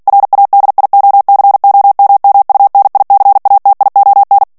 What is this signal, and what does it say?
Morse code, heard as a keyed tone, about 48 wpm, 800 Hz. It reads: RADIOCOMMUNICATION
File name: ANGLAIS texte clair 3 vitesse 45 mots minute